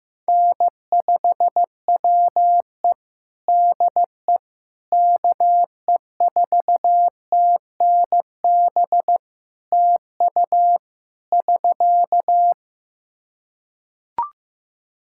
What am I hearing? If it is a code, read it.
Morse code, heard as a keyed tone, about 15 wpm, 700 Hz. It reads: N5WE DE KE4TNB TU <SK>
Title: K1USN SST QSO Practice 15wpm